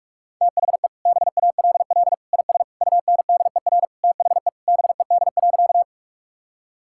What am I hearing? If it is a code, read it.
Morse code, heard as a keyed tone, about 46 wpm, 700 Hz. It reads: THE BALL IS UNDER THE BED.
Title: Sentences from Top 1000 Words - Review Part 6 - 45wpm